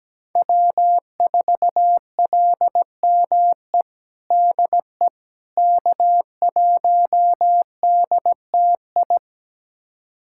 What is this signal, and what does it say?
Morse code, heard as a keyed tone, about 17 wpm, 700 Hz. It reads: W4LME DE K1DTI
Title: USA Callsign DE Callsign - 5 Characters - 17wpm